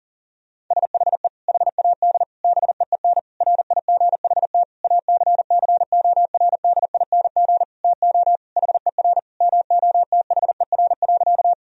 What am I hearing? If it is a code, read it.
Morse code, heard as a keyed tone, about 37 wpm, 700 Hz. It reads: SHE HAD BEEN RIGHT ACCORDING TO HER MOTHER.
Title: Sentences from Top 1000 Words - Review Part 4 - 40wpm